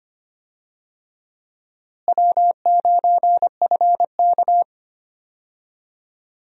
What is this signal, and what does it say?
Morse code, heard as a keyed tone, about 25 wpm, 700 Hz. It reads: W9FK
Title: USA Call Signs - 4 Characters - Repeated 2x - 25wpm